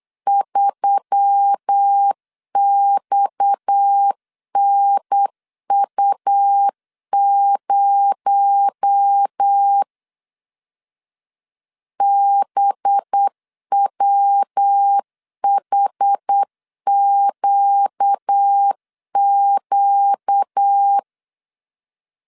Morse code, heard as a keyed tone, about 8 wpm, 800 Hz. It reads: 3XNU0 BWHQQ
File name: CW / DET / 1 6 1500C S B 8wpm 5142